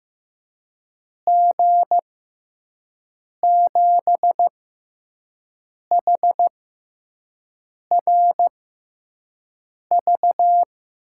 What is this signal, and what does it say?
Morse code, heard as a keyed tone, about 15 wpm, spaced as for 5 wpm, 700 Hz. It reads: G7HRV
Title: USA Call Signs - 6 Characters - 6x Character Spacing - 15wpm